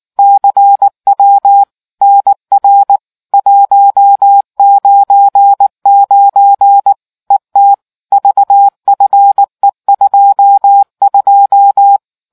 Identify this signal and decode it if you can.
Morse code, heard as a keyed tone, about 19 wpm, 800 Hz. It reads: CW NR 199 ET VFE22